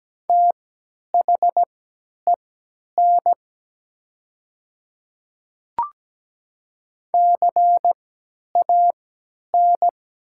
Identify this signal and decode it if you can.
Morse code, heard as a keyed tone, about 17 wpm, spaced as for 10 wpm, 700 Hz. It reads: THEN CAN
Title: 3x Character Spacing - Top 100 Words - 17wpm